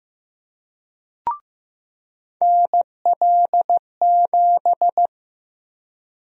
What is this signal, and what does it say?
Morse code, heard as a keyed tone, about 15 wpm, 700 Hz. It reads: NL7
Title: USA Call Sign Prefixes - 15wpm